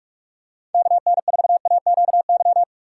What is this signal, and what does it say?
Morse code, heard as a keyed tone, about 40 wpm, 700 Hz. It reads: KN4AQY